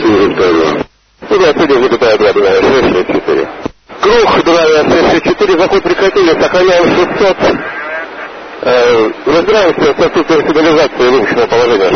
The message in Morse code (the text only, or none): none